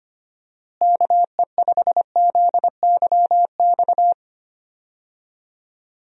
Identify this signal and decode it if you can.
Morse code, heard as a keyed tone, about 25 wpm, 700 Hz. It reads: KE5ZYX
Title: USA Call Signs - 6 Characters - Repeated 2x - 25wpm